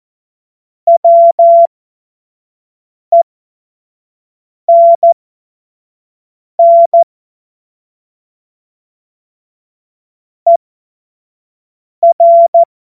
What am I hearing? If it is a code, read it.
Morse code, heard as a keyed tone, about 14 wpm, spaced as for 5 wpm, 680 Hz. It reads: WENN ER
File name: Morsekurs / Witze / Witz 02. November 2021, 25 F70